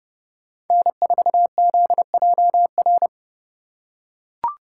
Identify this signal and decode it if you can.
Morse code, heard as a keyed tone, about 29 wpm, 700 Hz. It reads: N4ZJR